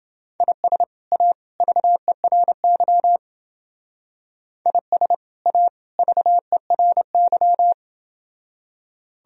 Morse code, heard as a keyed tone, about 30 wpm, 700 Hz. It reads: IS A VERY IS A VERY